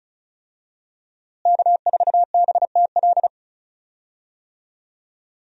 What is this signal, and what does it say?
Morse code, heard as a keyed tone, about 36 wpm, 700 Hz. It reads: K4BTL
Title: USA Call Signs - 5 Characters - Repeated 3x - 35wpm